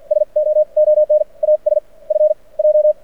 Morse code, heard as a keyed tone, about 50 wpm, spaced as for 43 wpm, 600 Hz. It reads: SO ON AS F J
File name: archived / CRSnet